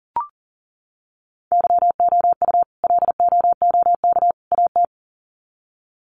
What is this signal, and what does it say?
Morse code, heard as a keyed tone, about 40 wpm, 700 Hz. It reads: YOU LOOK AT
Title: Sets of 3 Words 40wpm